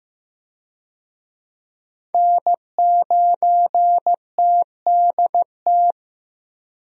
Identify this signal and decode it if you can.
Morse code, heard as a keyed tone, about 15 wpm, 700 Hz. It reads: N9TDT
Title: USA Call Signs - 5 Characters - Repeated 3x - 15wpm